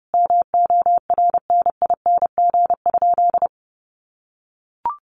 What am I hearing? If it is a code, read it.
Morse code, heard as a keyed tone, about 30 wpm, 700 Hz. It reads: MORNING?
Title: Sentences from Top 1200 Words 30wpm